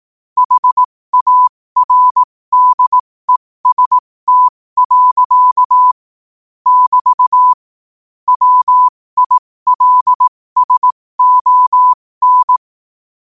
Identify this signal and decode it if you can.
Morse code, heard as a keyed tone, about 18 wpm, spaced as for 15 wpm, 1000 Hz. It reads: HARDEST. <BT> WILSON
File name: qotdfiles / 15wpm